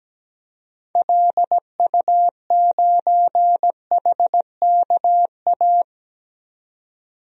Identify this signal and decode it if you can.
Morse code, heard as a keyed tone, about 17 wpm, 700 Hz. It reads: LU9HKA
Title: DXCC-100 Call Signs - 6 Characters - 17wpm